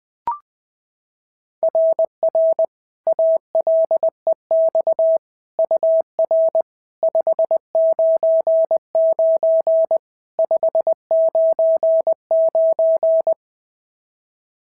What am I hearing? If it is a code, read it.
Morse code, heard as a keyed tone, about 20 wpm, 660 Hz. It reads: RR ALEX UR 599 599
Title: SOTA QSO Practice 20wpm